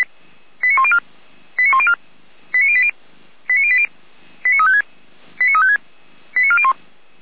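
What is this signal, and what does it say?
Five-tone selective call (ZVEI), heard as quick runs of five tones, tones of about 70 ms. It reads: 89104 89104 89080 89080 89356 89356 89401
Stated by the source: ZVEI five-tone selective call received on SDR.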